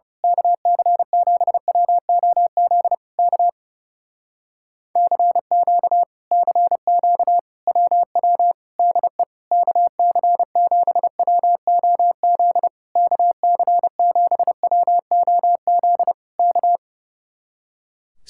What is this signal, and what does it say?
Morse code, heard as a keyed tone, about 28 wpm, 700 Hz. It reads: KC7WOZ K CQ CQ WW DE KC7WOZ KC7WOZ K
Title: Speed-Racing Common QSO Phrases Encore - 30wpm